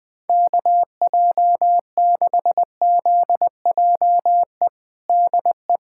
Morse code, heard as a keyed tone, about 20 wpm, 700 Hz. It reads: KJ6ZJE DE
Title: USA Callsign DE Callsign 20wpm